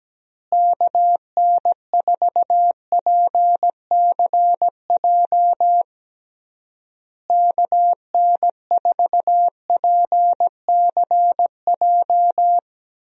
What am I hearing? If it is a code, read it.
Morse code, heard as a keyed tone, about 17 wpm, 700 Hz. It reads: KN4PCJ KN4PCJ
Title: USA Call Signs - 6 Characters - Repeated 3x - 17wpm